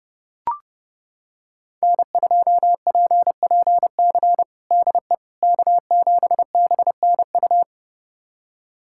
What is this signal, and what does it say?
Morse code, heard as a keyed tone, about 30 wpm, 700 Hz. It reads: N2PPC DE K7BNU
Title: USA Callsign DE Callsign - 5 Characters - 30wpm